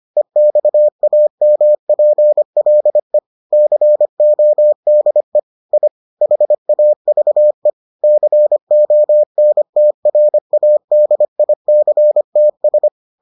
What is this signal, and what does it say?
Morse code, heard as a keyed tone, about 25 wpm, 600 Hz. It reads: EXAMPLE CODE I HAVE CONTRADICTS